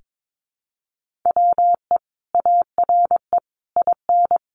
Morse code, heard as a keyed tone, about 22 wpm, 700 Hz. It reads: WE ARE IN
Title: Sentences from Top 1400 Words 22wpm